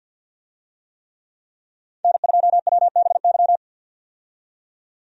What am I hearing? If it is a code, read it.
Morse code, heard as a keyed tone, about 52 wpm, 700 Hz. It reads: N2WBY
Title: USA Call Signs - 5 Characters - 50wpm